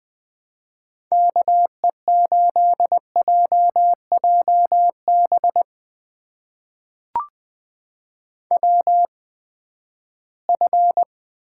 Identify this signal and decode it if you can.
Morse code, heard as a keyed tone, about 21 wpm, spaced as for 14 wpm, 700 Hz. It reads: KE8JJB WF
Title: USA Call Signs - 8x Character Spacing - 20wpm